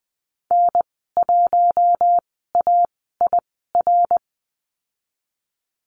Morse code, heard as a keyed tone, about 20 wpm, spaced as for 15 wpm, 700 Hz. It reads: N1AIR
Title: 128 Lesson 21g - Number 6 - Call Signs 20wpm